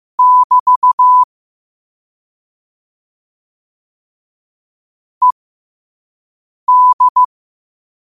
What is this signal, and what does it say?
Morse code, heard as a keyed tone, about 15 wpm, spaced as for 5 wpm, 1000 Hz. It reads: <BT> ED